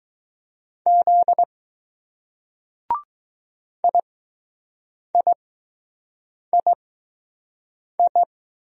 Morse code, heard as a keyed tone, about 20 wpm, spaced as for 6 wpm, 700 Hz. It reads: Z IIII